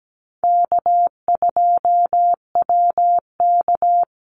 Morse code, heard as a keyed tone, about 17 wpm, 700 Hz. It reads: K2WK
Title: Call Signs - 17wpm